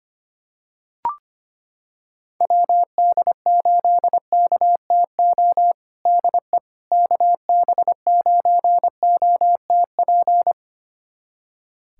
Morse code, heard as a keyed tone, about 25 wpm, 700 Hz. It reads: WD8KTO DE KB9OTP
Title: USA Callsign DE Callsign - 6 Characters - 25wpm